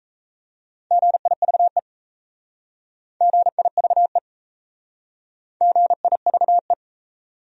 Morse code, heard as a keyed tone, about 38 wpm, 700 Hz. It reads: GIVE GIVE GIVE